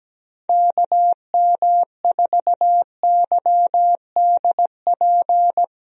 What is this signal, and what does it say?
Morse code, heard as a keyed tone, about 17 wpm, 700 Hz. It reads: KM4YDP
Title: USA Call Signs - 6 Characters - 17wpm